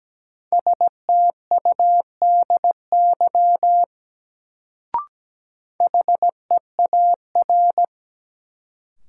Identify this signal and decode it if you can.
Morse code, heard as a keyed tone, about 17 wpm, 700 Hz. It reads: STUDY HEAR
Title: Sentences from Top 300 Words - New Word Drill - 17wpm